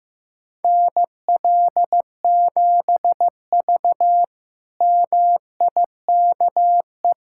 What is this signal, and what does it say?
Morse code, heard as a keyed tone, about 15 wpm, 700 Hz. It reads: NL7V MIKE
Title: K1USN SST QSO Practice - 15wpm